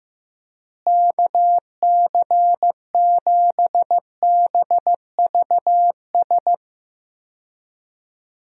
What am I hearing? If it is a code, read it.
Morse code, heard as a keyed tone, about 15 wpm, 700 Hz. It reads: KC7BVS
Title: USA Call Signs - 6 Characters - Repeated 2x - 15wpm